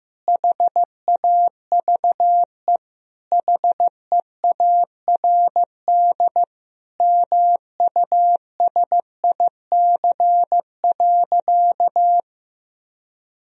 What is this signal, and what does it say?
Morse code, heard as a keyed tone, about 15 wpm, 700 Hz. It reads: HAVE HEARD MUSIC.
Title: Sentences from Top 1000 Words - Review Part 1 - 15wpm